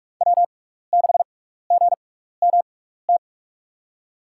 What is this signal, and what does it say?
Morse code, heard as a keyed tone, about 44 wpm, spaced as for 15 wpm, 700 Hz. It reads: W6GMT
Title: USA Call Signs - 5 Characters - 6x Character Spacing - 45wpm